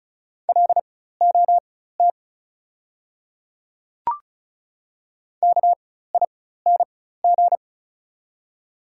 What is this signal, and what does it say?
Morse code, heard as a keyed tone, about 34 wpm, spaced as for 16 wpm, 690 Hz. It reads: LOT KING
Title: 4x Character Spacing - Top 500 Words - 35wpm